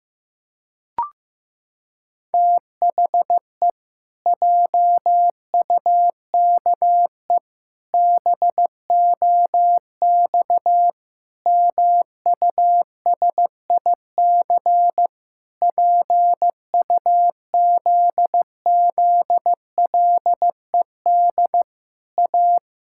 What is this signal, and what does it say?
Morse code, heard as a keyed tone, about 15 wpm, 700 Hz. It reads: THE JUKE BOX MUSIC PUZZLED A
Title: Pangram Challenge v2 - 15wpm